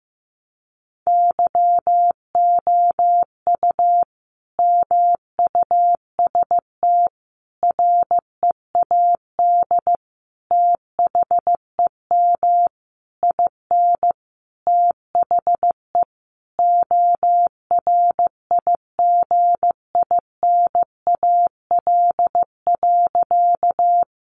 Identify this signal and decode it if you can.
Morse code, heard as a keyed tone, about 15 wpm, 700 Hz. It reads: YOU MUST READ THEM IN THE ORIGINAL.
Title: Sentences from Top 900 Words 15wpm